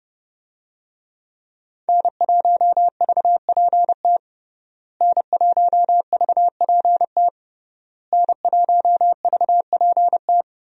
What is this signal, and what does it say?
Morse code, heard as a keyed tone, about 30 wpm, 700 Hz. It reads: N1VPT N1VPT N1VPT
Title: USA Call Signs - 5 Characters - Repeated 3x - 30wpm